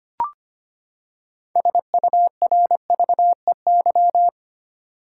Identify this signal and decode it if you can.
Morse code, heard as a keyed tone, about 25 wpm, 700 Hz. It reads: SURVEY
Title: Six Letter Words - 25wpm